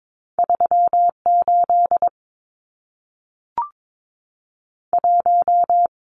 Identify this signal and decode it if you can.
Morse code, heard as a keyed tone, about 22 wpm, 700 Hz. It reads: 38 1
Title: Numbers - 2-Digit 22wpm